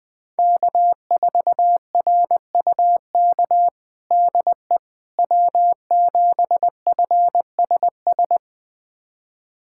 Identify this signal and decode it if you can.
Morse code, heard as a keyed tone, about 20 wpm, 700 Hz. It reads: K4RUK DE W7FSS
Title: USA Callsign DE Callsign - 5 Characters - 20wpm